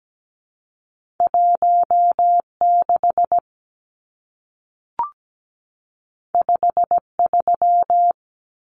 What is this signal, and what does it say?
Morse code, heard as a keyed tone, about 17 wpm, 700 Hz. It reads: 16 53
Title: Numbers - 2-Digit 17wpm